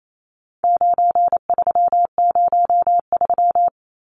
Morse code, heard as a keyed tone, about 28 wpm, 700 Hz. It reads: 9303
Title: Numbers - 4-Digit 28wpm